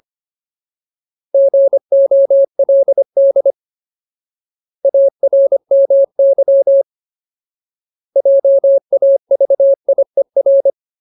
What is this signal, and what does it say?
Morse code, heard as a keyed tone, about 25 wpm, 550 Hz. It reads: GOLD ARMY JAVIER
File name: Ultimate Word List / 25WPM Ultimate